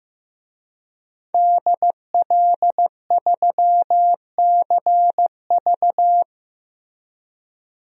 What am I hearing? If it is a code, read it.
Morse code, heard as a keyed tone, about 15 wpm, 700 Hz. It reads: DL3CV
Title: DXCC-100 Call Signs - 5 Characters - 15wpm